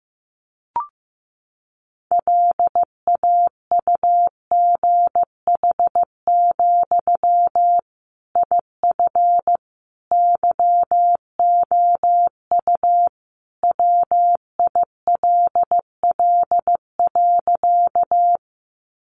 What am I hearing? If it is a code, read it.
Morse code, heard as a keyed tone, about 15 wpm, 700 Hz. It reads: LAUGH, IF YOU WILL.
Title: Sentences from Top 500 Words 15wpm